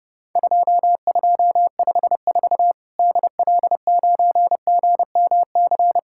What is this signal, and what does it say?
Morse code, heard as a keyed tone, about 30 wpm, 700 Hz. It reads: 2254 DL9GMC
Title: Worked All Europe DX Contest - QTC Exchange - 30wpm